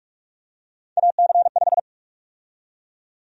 Morse code, heard as a keyed tone, about 42 wpm, 700 Hz. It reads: AC5
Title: USA Call Sign Prefixes - 45wpm